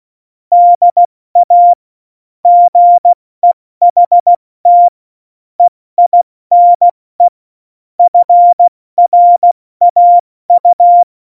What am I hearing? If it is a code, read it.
Morse code, heard as a keyed tone, about 16 wpm, 700 Hz. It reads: DA GEHT EINE FRAU
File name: Morsekurs / Witze / Witz 09. Januar 2018, 70 F80